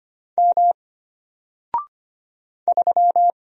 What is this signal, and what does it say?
Morse code, heard as a keyed tone, about 25 wpm, 700 Hz. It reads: M 3
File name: Single-Letter-Number-25wpm